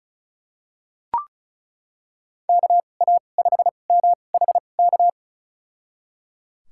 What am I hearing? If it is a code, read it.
Morse code, heard as a keyed tone, about 34 wpm, spaced as for 25 wpm, 700 Hz. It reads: KA5MHK
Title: USA Call Signs - 2x Character Spacing - 35wpm